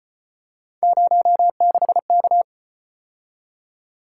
Morse code, heard as a keyed tone, about 35 wpm, 700 Hz. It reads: 06K